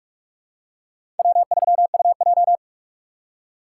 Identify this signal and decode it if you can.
Morse code, heard as a keyed tone, about 46 wpm, 700 Hz. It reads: W3UJ
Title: USA Call Signs - 4 Characters - 6x Character Spacing - 45wpm